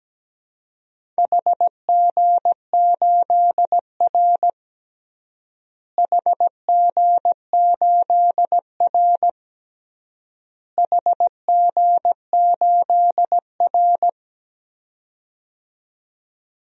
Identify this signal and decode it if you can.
Morse code, heard as a keyed tone, about 17 wpm, 700 Hz. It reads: HG8R HG8R HG8R
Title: DXCC-100 Call Signs - 4 Characters - Repeated 3x - 17wpm